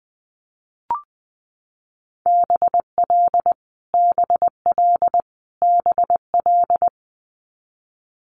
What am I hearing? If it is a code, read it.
Morse code, heard as a keyed tone, about 20 wpm, 700 Hz. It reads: BL BL BL
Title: Multi-letter Phonemes Repeated 3x 20wpm